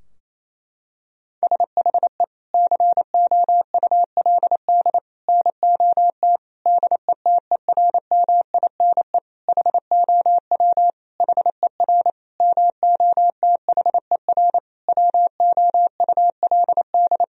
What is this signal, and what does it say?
Morse code, heard as a keyed tone, about 28 wpm, 700 Hz. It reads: SHE COULD NOT DETERMINE HOW HER MOTHER WOULD
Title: Sentences from Top 1000 Words - Review Part 4 - 28wpm